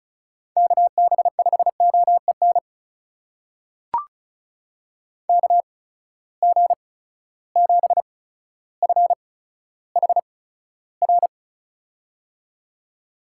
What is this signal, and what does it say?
Morse code, heard as a keyed tone, about 36 wpm, spaced as for 15 wpm, 700 Hz. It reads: KB5OEN KG7FHR
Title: USA Call Signs - 8x Character Spacing - 35wpm